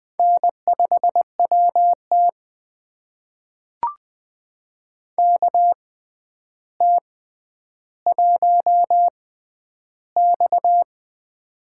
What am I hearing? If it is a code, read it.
Morse code, heard as a keyed tone, about 20 wpm, spaced as for 10 wpm, 700 Hz. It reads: N5WT KT1X
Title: USA Call Signs - 4 Characters - 6x Character Spacing - 20wpm